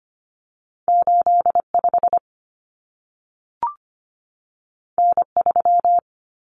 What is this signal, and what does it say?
Morse code, heard as a keyed tone, about 25 wpm, 700 Hz. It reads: 85 N3
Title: Common Cut Numbers - 2-Digit 25wpm